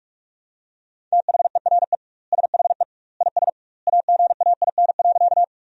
Morse code, heard as a keyed tone, about 41 wpm, 700 Hz. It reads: THERE SHE IS AGAIN.
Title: 2x Word-Spacing - Sentences from Top 200 Words - 45wpm